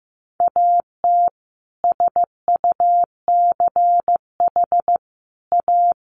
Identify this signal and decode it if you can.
Morse code, heard as a keyed tone, about 15 wpm, 700 Hz. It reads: AT SUCH A
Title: Sentences from Top 1000 Words 15wpm